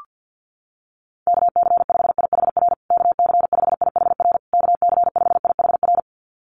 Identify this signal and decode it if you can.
Morse code, heard as a keyed tone, about 49 wpm, 700 Hz. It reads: KC5IHR KC5IHR KC5IHR
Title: USA Call Signs Repeated 3x 50wpm